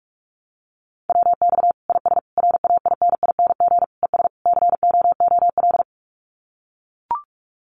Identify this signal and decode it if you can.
Morse code, heard as a keyed tone, about 44 wpm, 700 Hz. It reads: WX IS RAINING ES COOL
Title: Common QSO Phrases 45wpm